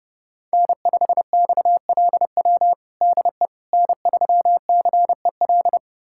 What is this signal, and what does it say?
Morse code, heard as a keyed tone, about 30 wpm, 700 Hz. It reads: N5XLW DE N3CEL
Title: USA Callsign DE Callsign 30wpm